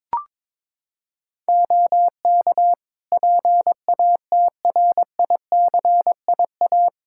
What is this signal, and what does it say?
Morse code, heard as a keyed tone, about 23 wpm, 700 Hz. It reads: OK PATRICIA
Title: Practice QSO - Exchange 2 - Info - 22wpm